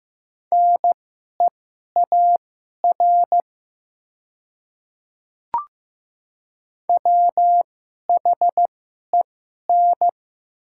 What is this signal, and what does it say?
Morse code, heard as a keyed tone, about 15 wpm, spaced as for 11 wpm, 700 Hz. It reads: NEAR WHEN
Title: 2x Character Spacing - Top 200 Words - 15wpm